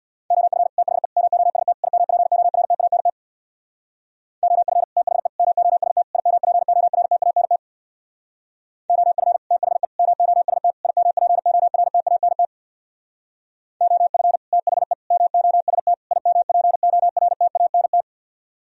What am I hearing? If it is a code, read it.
Morse code, heard as a keyed tone, about 65 wpm, 700 Hz. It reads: OF THE MOST IMPORTANT OF THE MOST IMPORTANT OF THE MOST IMPORTANT OF THE MOST IMPORTANT